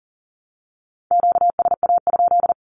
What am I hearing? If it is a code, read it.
Morse code, heard as a keyed tone, about 40 wpm, 700 Hz. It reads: QSA?
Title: Q-Codes 40wpm